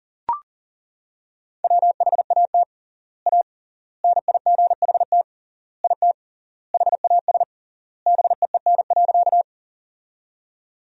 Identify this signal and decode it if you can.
Morse code, heard as a keyed tone, about 40 wpm, 700 Hz. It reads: WHAT A NIGHT IT HAS BEEN.
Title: 3x Word-Spacing - Sentences from Top 300 Words - 40wpm